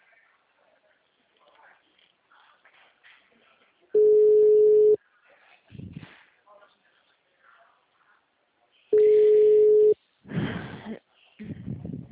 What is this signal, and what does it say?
Ringback tone (Europe, 425 Hz), with speech.